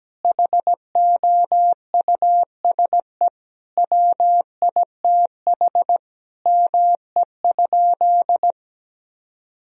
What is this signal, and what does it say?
Morse code, heard as a keyed tone, about 17 wpm, 700 Hz. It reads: HOUSE WITH ME?
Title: Sentences from Top 1000 Words - Review Part 1 - 17wpm